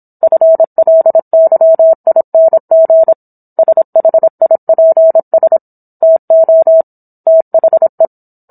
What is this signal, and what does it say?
Morse code, heard as a keyed tone, about 27 wpm, 650 Hz. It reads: FLYING SHIPS TO THE